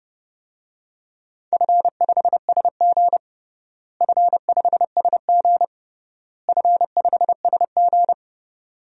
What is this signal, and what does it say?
Morse code, heard as a keyed tone, about 30 wpm, 700 Hz. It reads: F5SG F5SG F5SG
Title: DXCC-100 Call Signs - 4 Characters - Repeated 3x - 30wpm